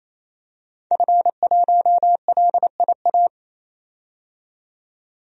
Morse code, heard as a keyed tone, about 28 wpm, 700 Hz. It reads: F1LIA
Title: DXCC-100 Call Signs - 5 Characters - 28wpm